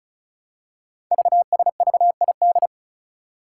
Morse code, heard as a keyed tone, about 36 wpm, 700 Hz. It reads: VSVID
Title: Code Groups - 5 Letters - 35wpm